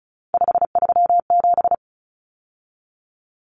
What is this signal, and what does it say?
Morse code, heard as a keyed tone, about 34 wpm, 700 Hz. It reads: 537
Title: Common Cut Numbers - 3-Digit 35wpm